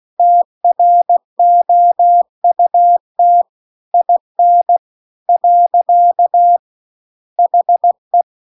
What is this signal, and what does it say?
Morse code, heard as a keyed tone, about 16 wpm, 700 Hz. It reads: TROUT IN . HE
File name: CW-The-Big-Twoheaded-River-2-16WPM